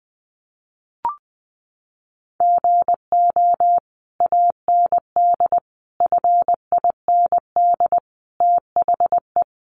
Morse code, heard as a keyed tone, about 20 wpm, 700 Hz. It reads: GO AND FIND THE
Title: Sentences from Top 1200 Words 20wpm